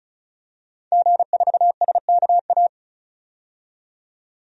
Morse code, heard as a keyed tone, about 34 wpm, 700 Hz. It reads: G4SKA
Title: DXCC-100 Call Signs - 5 Characters - 35wpm